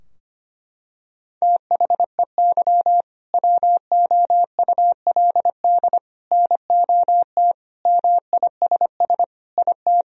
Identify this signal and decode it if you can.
Morse code, heard as a keyed tone, about 25 wpm, 700 Hz. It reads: THEY WOULD NOT MISS IT
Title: Sentences from Top 1000 Words - Review Part 3 - 25wpm